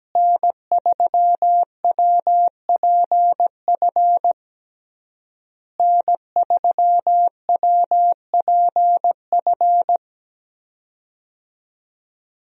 Morse code, heard as a keyed tone, about 17 wpm, 700 Hz. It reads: N3WPF N3WPF
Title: USA Call Signs - 5 Characters - Repeated 3x - 17wpm